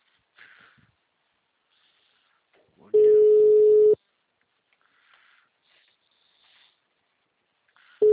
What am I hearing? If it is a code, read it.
Ringback tone (Europe, 425 Hz).
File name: out- -2007- - - .587